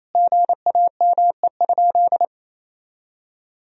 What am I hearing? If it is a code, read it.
Morse code, heard as a keyed tone, about 28 wpm, 700 Hz. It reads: GAME?
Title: Sentences from Top 1000 Words - Review Part 4 - 28wpm